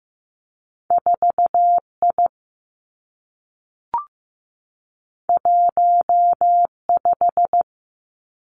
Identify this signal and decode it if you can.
Morse code, heard as a keyed tone, about 15 wpm, 700 Hz. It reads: 4 I 1 5
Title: Two Challenging Letter-Number Combinations 15wpm